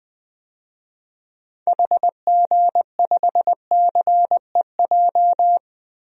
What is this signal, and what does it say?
Morse code, heard as a keyed tone, about 20 wpm, 700 Hz. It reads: HG5CEJ
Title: DXCC-100 Call Signs - 6 Characters - Repeated 3x - 20wpm